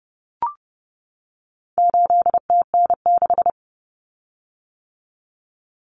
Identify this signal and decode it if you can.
Morse code, heard as a keyed tone, about 30 wpm, 700 Hz. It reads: 8TN6
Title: Common Cut Numbers - 4-Digit 30wpm